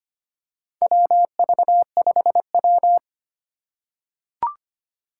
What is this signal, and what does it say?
Morse code, heard as a keyed tone, about 25 wpm, 690 Hz. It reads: WV5W
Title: USA Call Signs - 4 Characters - 4x Character Spacing - 25wpm